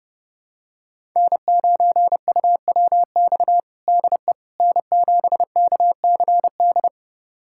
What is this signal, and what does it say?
Morse code, heard as a keyed tone, about 32 wpm, 700 Hz. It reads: N9UWX DE N7KCD
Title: USA Callsign DE Callsign - 5 Characters - 30wpm